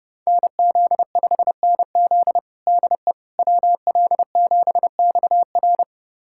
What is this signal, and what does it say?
Morse code, heard as a keyed tone, about 32 wpm, 700 Hz. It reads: NZ5NZ DE WL7XR
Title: USA Callsign DE Callsign - 5 Characters - 30wpm